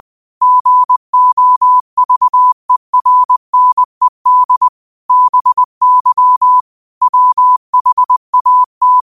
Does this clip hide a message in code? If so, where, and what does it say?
Yes: Morse code from 0.4 to 9.0 s, heard as a keyed tone, about 20 wpm, 1000 Hz. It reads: GOVERNED BY WHAT